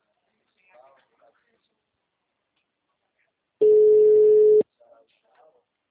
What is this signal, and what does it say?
Ringback tone (Europe, 425 Hz), with speech.
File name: out- -2005- - - .64428